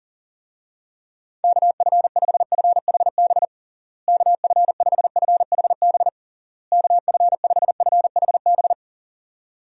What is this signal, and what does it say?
Morse code, heard as a keyed tone, about 40 wpm, 700 Hz. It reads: KF5FHB KF5FHB KF5FHB
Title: USA Call Signs - 6 Characters - Repeated 3x - 40wpm